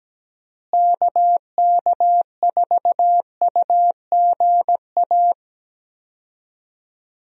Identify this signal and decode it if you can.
Morse code, heard as a keyed tone, about 17 wpm, 700 Hz. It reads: KK4UGA
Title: USA Call Signs - 6 Characters - Repeated 3x - 17wpm